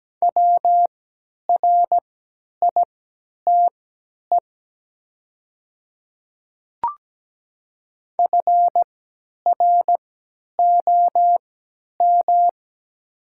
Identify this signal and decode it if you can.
Morse code, heard as a keyed tone, about 17 wpm, spaced as for 10 wpm, 700 Hz. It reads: WRITE FROM
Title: 3x Character Spacing - Top 100 Words - 17wpm